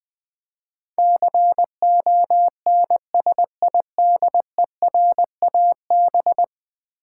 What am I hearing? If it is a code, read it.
Morse code, heard as a keyed tone, about 20 wpm, 700 Hz. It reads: CONSIDERAB
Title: Build A Word - 5 Letters At A Time - 12-Letter Words - 20wpm